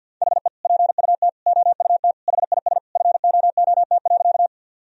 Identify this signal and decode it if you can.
Morse code, heard as a keyed tone, about 50 wpm, 700 Hz. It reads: HE PUT OUT HIS FOOT.
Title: Sentences from Top 1000 Words - Review Part 3 - 50wpm